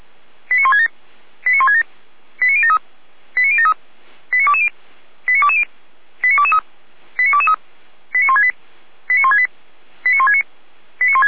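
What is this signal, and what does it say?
Five-tone selective call (ZVEI), heard as quick runs of five tones, tones of about 70 ms. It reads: 89167 89167 89073 89073 89220 89220 89223 89223 89168 89168 89169
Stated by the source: Five-tone ZVEI selective call received with an SDR.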